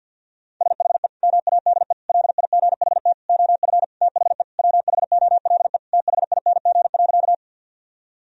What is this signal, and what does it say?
Morse code, heard as a keyed tone, about 53 wpm, 700 Hz. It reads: SHE MADE LIGHT OF THE WHOLE THING.
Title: Sentences from Top 1000 Words - Review Part 1 - 50wpm